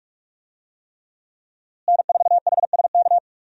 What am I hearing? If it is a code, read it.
Morse code, heard as a keyed tone, about 41 wpm, 700 Hz. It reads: N4HSK
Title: USA Call Signs - 5 Characters - Repeated 3x - 45wpm